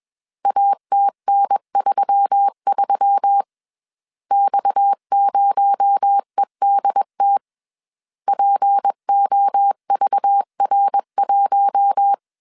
Morse code, heard as a keyed tone, about 21 wpm, 800 Hz. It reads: ATN33 X0EDT POVR1
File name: CW / DET / 1 6 1500C S C 20wpm 2116